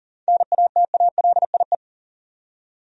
Morse code, heard as a keyed tone, about 40 wpm, 700 Hz. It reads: NATALIE
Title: USA Common Names - 40wpm